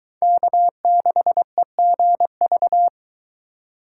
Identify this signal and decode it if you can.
Morse code, heard as a keyed tone, about 23 wpm, 700 Hz. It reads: K6EGV